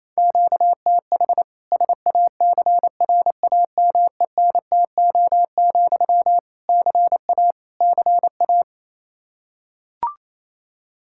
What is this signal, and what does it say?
Morse code, heard as a keyed tone, about 28 wpm, 700 Hz. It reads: QTH SACRAMENTO, CA CA
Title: K1USN SST QSO Practice 28wpm